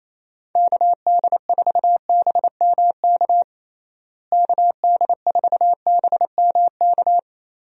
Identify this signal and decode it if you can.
Morse code, heard as a keyed tone, about 28 wpm, 700 Hz. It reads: KD4BMK KD4BMK
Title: USA Call Signs - 6 Characters - Repeated 3x - 28wpm